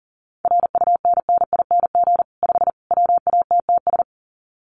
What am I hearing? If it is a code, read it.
Morse code, heard as a keyed tone, about 40 wpm, 700 Hz. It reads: RUNNING 5 WATTS
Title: Common QSO Phrases 40wpm